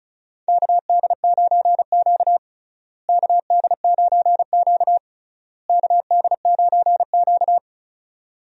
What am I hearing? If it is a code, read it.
Morse code, heard as a keyed tone, about 34 wpm, 700 Hz. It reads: KD9Q KD9Q KD9Q
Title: USA Call Signs - 4 Characters - Repeated 3x - 35wpm